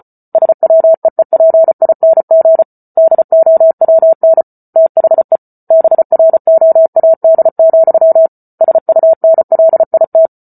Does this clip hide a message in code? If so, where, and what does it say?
Yes: Morse code from 0.3 to 10.3 s, heard as a keyed tone, about 34 wpm, 650 Hz. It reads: SWEEPING DOWN THE BROAD, SUNLIT